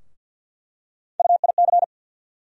Morse code, heard as a keyed tone, about 47 wpm, 700 Hz. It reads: UIC
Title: USA Call Sign Postfixes - 3 Letters 50wpm